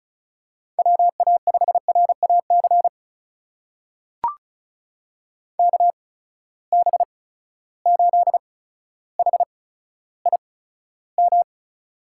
Morse code, heard as a keyed tone, about 36 wpm, spaced as for 15 wpm, 700 Hz. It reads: WA5RAC KB8HIM
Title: USA Call Signs - 6 Characters - 8x Character Spacing - 35wpm